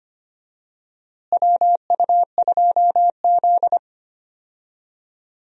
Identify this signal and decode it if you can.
Morse code, heard as a keyed tone, about 25 wpm, 700 Hz. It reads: WU2Z
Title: USA Call Signs - 4 Characters - Repeated 3x - 25wpm